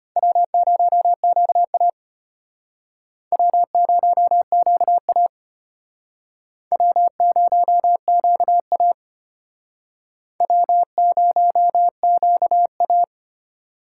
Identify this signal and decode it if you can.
Morse code, heard as a keyed tone, about 33 wpm, 700 Hz. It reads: W0QA W0QA W0QA W0QA